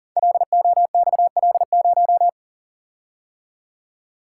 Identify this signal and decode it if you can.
Morse code, heard as a keyed tone, about 42 wpm, 700 Hz. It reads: LOXL0
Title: Code Groups - 5 Letter:Numbers - 40wpm